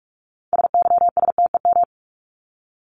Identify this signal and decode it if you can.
Morse code, heard as a keyed tone, about 44 wpm, 700 Hz. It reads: SYSTEM
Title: ES:S Suffix - Top 100 Words with Base Words 45wpm